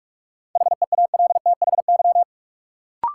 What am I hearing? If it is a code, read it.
Morse code, heard as a keyed tone, about 46 wpm, 700 Hz. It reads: HEALTHY
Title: Sentences from Top 1000 Words - Review Part 5 - New Word Drill - 45wpm